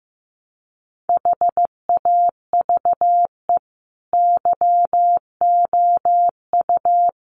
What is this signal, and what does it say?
Morse code, heard as a keyed tone, about 15 wpm, 700 Hz. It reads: HAVE YOU
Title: Sentences from Top 1200 Words 15wpm